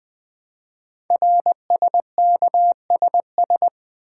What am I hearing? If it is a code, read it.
Morse code, heard as a keyed tone, about 20 wpm, 700 Hz. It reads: RSKSS
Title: Code Groups - 5 Letters - 20wpm